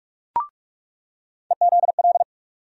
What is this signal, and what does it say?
Morse code, heard as a keyed tone, about 43 wpm, 700 Hz. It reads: EZL
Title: USA Call Sign Postfixes - 3 Letters 45wpm